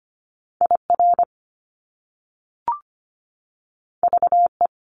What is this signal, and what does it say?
Morse code, heard as a keyed tone, about 25 wpm, 700 Hz. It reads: IR VE
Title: Top Two Letter Combinations in Words 25wpm